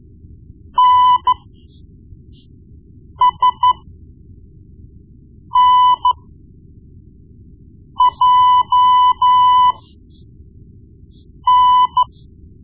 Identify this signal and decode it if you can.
Morse code, heard as a keyed tone, about 13 wpm, spaced as for 4 wpm, 1000 Hz. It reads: NSNJN
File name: Morse Lessons / Morse Lessons / Tape-1B